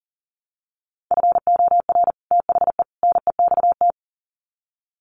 Morse code, heard as a keyed tone, about 40 wpm, 700 Hz. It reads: FOR THE NEXT
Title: Sets of 3 Words 40wpm